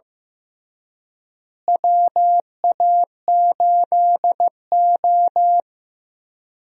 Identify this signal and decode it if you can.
Morse code, heard as a keyed tone, about 15 wpm, 700 Hz. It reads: WA8O
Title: USA Call Signs - 4 Characters - Repeated 3x - 15wpm